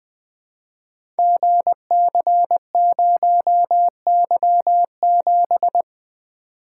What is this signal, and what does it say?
Morse code, heard as a keyed tone, about 20 wpm, 700 Hz. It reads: GC0Y7
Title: Code Groups - 5 Letter:Numbers - 20wpm